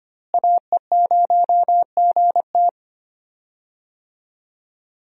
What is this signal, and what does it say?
Morse code, heard as a keyed tone, about 25 wpm, 700 Hz. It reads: AE0GT
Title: USA Call Signs - 5 Characters - Repeated 2x - 25wpm